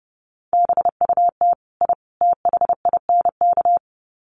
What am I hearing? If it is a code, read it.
Morse code, heard as a keyed tone, about 30 wpm, 700 Hz. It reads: BUT I THINK
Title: Sets of 3 Words 30wpm